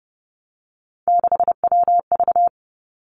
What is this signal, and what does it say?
Morse code, heard as a keyed tone, about 30 wpm, 700 Hz. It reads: 6WV
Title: Three Challenging Letter-Number Combinations 30wpm